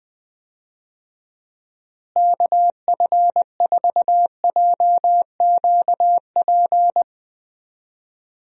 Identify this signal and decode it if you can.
Morse code, heard as a keyed tone, about 20 wpm, 700 Hz. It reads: KF4JQP
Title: USA Call Signs - 6 Characters - Repeated 2x - 20wpm